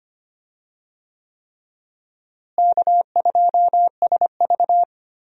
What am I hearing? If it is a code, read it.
Morse code, heard as a keyed tone, about 25 wpm, 700 Hz. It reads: K2SV
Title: USA Call Signs - 4 Characters - Repeated 3x - 25wpm